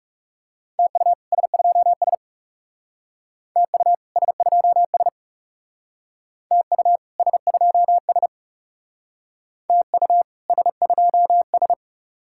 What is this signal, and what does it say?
Morse code, heard as a keyed tone, about 35 wpm, 700 Hz. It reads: TU S2S TU S2S TU S2S TU S2S
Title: Speed-Racing Common QSO Phrases - 30wpm